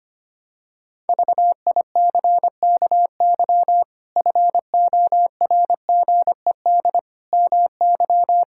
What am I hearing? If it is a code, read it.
Morse code, heard as a keyed tone, about 25 wpm, 700 Hz. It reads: VICKY FORGED MY
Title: Pangram Challenge v2 - 25wpm